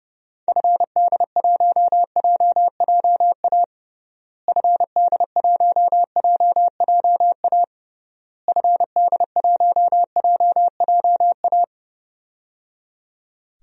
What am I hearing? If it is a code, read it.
Morse code, heard as a keyed tone, about 30 wpm, 700 Hz. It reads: FD1JJA FD1JJA FD1JJA
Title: DXCC-100 Call Signs - 6 Characters - Repeated 3x - 30wpm